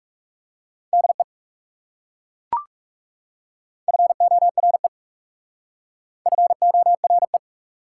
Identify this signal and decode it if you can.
Morse code, heard as a keyed tone, about 38 wpm, 700 Hz. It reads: DE FORE FORE